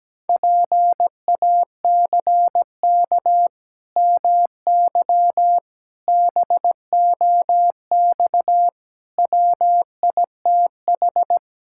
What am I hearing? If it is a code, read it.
Morse code, heard as a keyed tone, about 17 wpm, 700 Hz. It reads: PACK MY BOX WITH
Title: Pangram Challenge v2 - 17wpm